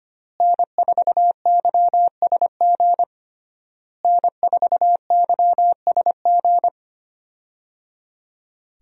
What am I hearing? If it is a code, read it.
Morse code, heard as a keyed tone, about 25 wpm, 700 Hz. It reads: N4YSG N4YSG
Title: USA Call Signs - 5 Characters - Repeated 3x - 25wpm